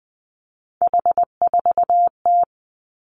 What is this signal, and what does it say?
Morse code, heard as a keyed tone, about 19 wpm, 700 Hz. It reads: H4T
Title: Three Challenging Letter-Number Combinations 20wpm